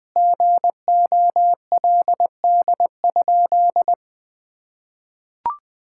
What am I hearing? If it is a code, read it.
Morse code, heard as a keyed tone, about 20 wpm, 700 Hz. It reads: GOLD?
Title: Sentences from Top 1600 Words 20wpm